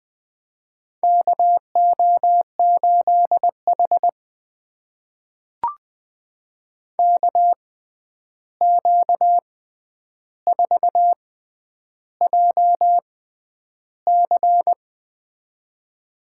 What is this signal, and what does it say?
Morse code, heard as a keyed tone, about 20 wpm, spaced as for 10 wpm, 700 Hz. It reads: KO8H KQ4JC
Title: USA Call Signs - 6x Character Spacing - 20wpm